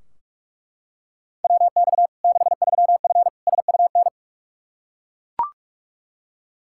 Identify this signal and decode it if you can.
Morse code, heard as a keyed tone, about 41 wpm, 700 Hz. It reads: WX 63F SUN
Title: Practice QSO - Exchange 2 - Info - 45wpm